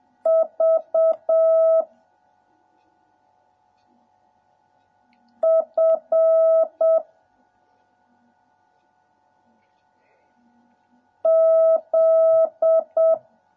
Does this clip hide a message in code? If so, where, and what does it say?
Yes: Morse code from 0.3 to 13.1 s, heard as a keyed tone, about 7 wpm, 650 Hz. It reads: V F Z